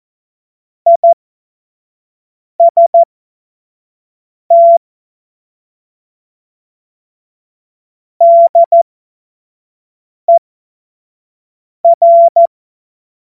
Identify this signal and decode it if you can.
Morse code, heard as a keyed tone, about 14 wpm, spaced as for 5 wpm, 680 Hz. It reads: IST DER
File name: Morsekurs / Witze / Witz 17 Juni 2025 25 F70